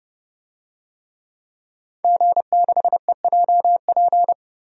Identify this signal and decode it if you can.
Morse code, heard as a keyed tone, about 32 wpm, 700 Hz. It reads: G6EJP
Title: DXCC-100 Call Signs - 5 Characters - 30wpm